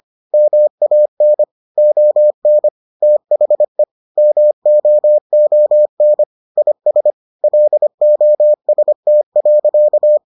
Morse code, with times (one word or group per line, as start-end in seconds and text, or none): 0.3-1.4 MAN
1.8-2.7 ON
3.0-3.8 THE
4.2-6.2 MOON
6.6-7.1 IS
7.4-10.2 LOST.